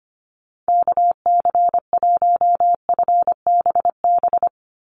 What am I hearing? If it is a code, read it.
Morse code, heard as a keyed tone, about 25 wpm, 700 Hz. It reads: KC1FBB
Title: USA Call Signs 25wpm